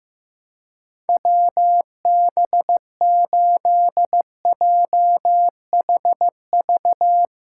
Morse code, heard as a keyed tone, about 15 wpm, 700 Hz. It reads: WB8JHV
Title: USA Callsign DE Callsign 15wpm